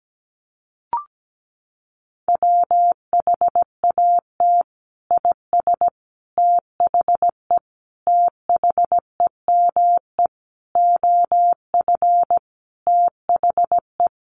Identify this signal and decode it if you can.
Morse code, heard as a keyed tone, about 17 wpm, 700 Hz. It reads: WHAT IS THE THEME OF THE
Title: Sentences from Top 1100 Words 17wpm